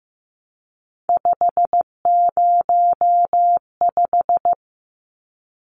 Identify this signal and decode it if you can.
Morse code, heard as a keyed tone, about 15 wpm, 700 Hz. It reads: 505
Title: Numbers - 3-Digit 15wpm